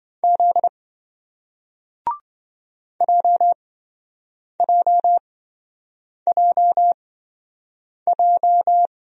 Morse code, heard as a keyed tone, about 27 wpm, spaced as for 7 wpm, 700 Hz. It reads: Z JJJJ